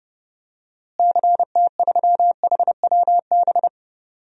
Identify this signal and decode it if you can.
Morse code, heard as a keyed tone, about 31 wpm, 700 Hz. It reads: CT3HWB
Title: DXCC-100 Call Signs - 6 Characters - Repeated 3x - 30wpm